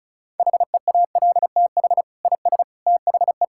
Morse code, heard as a keyed tone, about 36 wpm, 700 Hz. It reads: HEALTH IS THE
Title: Sentences from Top 1000 Words - Review Part 5 - 35wpm